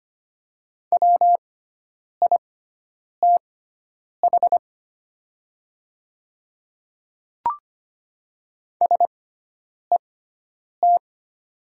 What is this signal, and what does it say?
Morse code, heard as a keyed tone, about 25 wpm, spaced as for 9 wpm, 700 Hz. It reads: WITH SET
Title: 6x Character Spacing - Top 300 Words - 25wpm